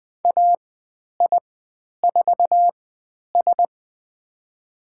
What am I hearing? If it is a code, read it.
Morse code, heard as a keyed tone, about 20 wpm, spaced as for 10 wpm, 700 Hz. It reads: AI4S
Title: 102 Lesson 17g - Letter u - Call Signs 20wpm